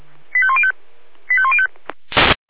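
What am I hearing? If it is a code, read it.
Five-tone selective call (ZVEI), heard as quick runs of five tones, tones of about 70 ms. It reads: 85105 85105
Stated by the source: ZVEI five-tone selective call received by SDR.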